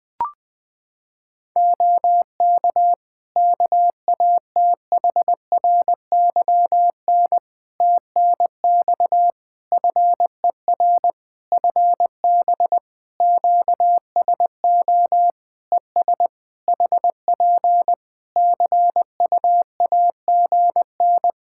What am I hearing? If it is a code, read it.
Morse code, heard as a keyed tone, about 20 wpm, 700 Hz. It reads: OK KATHRYN TNX FER FB QSO ES HP CUAGN
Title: Practice QSO - Exchange 4 - Ending - 20wpm